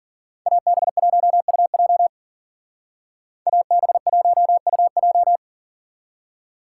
Morse code, heard as a keyed tone, about 52 wpm, 700 Hz. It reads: AB1UJ AB1UJ